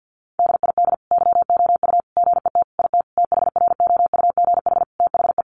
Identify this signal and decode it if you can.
Morse code, heard as a keyed tone, about 50 wpm, 700 Hz. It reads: DID YOU GET IT THROUGH THE
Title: Sentences from Top 1300 Words 50wpm